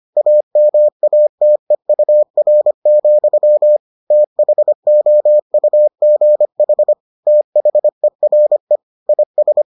Morse code, heard as a keyed tone, about 25 wpm, 600 Hz. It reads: AMATEUR, THOUGH THERE IS